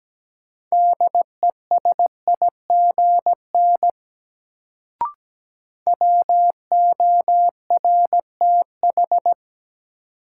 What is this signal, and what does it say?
Morse code, heard as a keyed tone, about 17 wpm, 700 Hz. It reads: DESIGN WORTH
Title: Sentences from Top 800 Words - New Word Drill - 17wpm